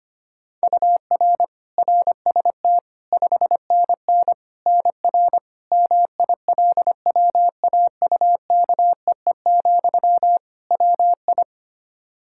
Morse code, heard as a keyed tone, about 25 wpm, 700 Hz. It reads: UR RST 5NN NR MILWAUKEE, WI
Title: Practice QSO - Exchange 1 - Beginning - 25wpm